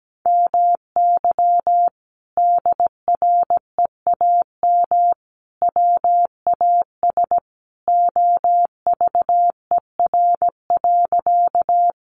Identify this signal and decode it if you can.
Morse code, heard as a keyed tone, about 17 wpm, 700 Hz. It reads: MY DREAM WAS OVER.
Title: Sentences from Top 1500 Words 17wpm